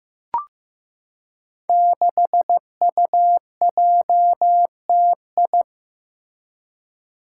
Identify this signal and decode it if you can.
Morse code, heard as a keyed tone, about 15 wpm, 700 Hz. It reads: 6UJTI
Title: Five Challenging Letter-Number Combinations 15wpm